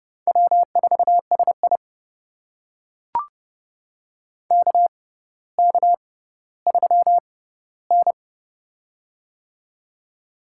Morse code, heard as a keyed tone, about 30 wpm, spaced as for 15 wpm, 700 Hz. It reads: W4SI KK3N
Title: USA Call Signs - 4 Characters - 6x Character Spacing - 30wpm